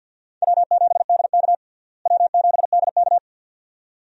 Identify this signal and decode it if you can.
Morse code, heard as a keyed tone, about 52 wpm, 700 Hz. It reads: W7DK W7DK
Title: USA Call Signs - 4 Characters - Repeated 2x - 50wpm